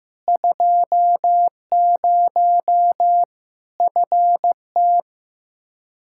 Morse code, heard as a keyed tone, about 15 wpm, 700 Hz. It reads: 20 FT
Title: Practice QSO - Exchange 2 - Info - 15wpm